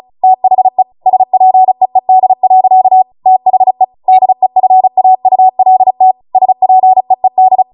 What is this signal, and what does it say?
Morse code, heard as a keyed tone, about 36 wpm, 750 Hz. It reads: THE SPEED. THE DEFAULT SPEED